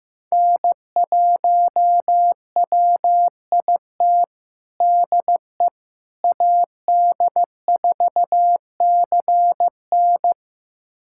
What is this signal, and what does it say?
Morse code, heard as a keyed tone, about 15 wpm, 700 Hz. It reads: N1WIT DE AD4CN
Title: USA Callsign DE Callsign - 5 Characters - 15wpm